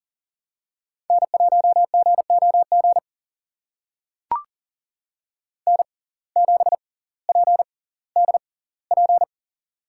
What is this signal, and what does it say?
Morse code, heard as a keyed tone, about 40 wpm, spaced as for 20 wpm, 700 Hz. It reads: N1GOG N7PDP
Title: USA Call Signs - 5 Characters - 6x Character Spacing - 40wpm